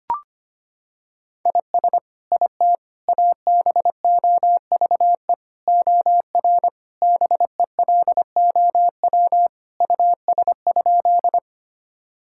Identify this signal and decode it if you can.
Morse code, heard as a keyed tone, about 25 wpm, 700 Hz. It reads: IS IT ABOVE OR BELOW US?
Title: Sentences from Top 1000 Words - Review Part 5 - 25wpm